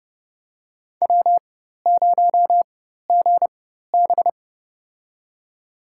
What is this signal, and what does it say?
Morse code, heard as a keyed tone, about 32 wpm, spaced as for 14 wpm, 700 Hz. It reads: W0GB
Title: USA Call Signs - 4x Character Spacing - 30wpm